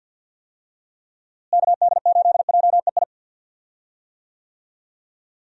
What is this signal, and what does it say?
Morse code, heard as a keyed tone, about 51 wpm, 700 Hz. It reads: KD8JEI
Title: USA Call Signs - 6 Characters - Repeated 3x - 50wpm